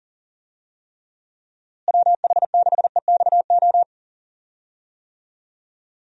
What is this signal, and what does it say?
Morse code, heard as a keyed tone, about 40 wpm, 700 Hz. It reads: WH6EXO
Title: USA Call Signs - 6 Characters - Repeated 3x - 40wpm